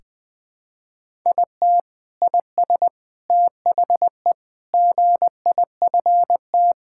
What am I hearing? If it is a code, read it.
Morse code, heard as a keyed tone, about 20 wpm, 700 Hz. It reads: IT IS THE GIFT
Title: Sentences from Top 1000 Words - Review Part 6 - 20wpm